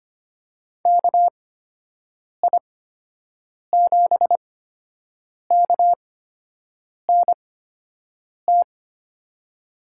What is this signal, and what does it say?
Morse code, heard as a keyed tone, about 25 wpm, spaced as for 7 wpm, 700 Hz. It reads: KI7KNT
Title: USA Call Signs - 6 Characters - 8x Character Spacing - 25wpm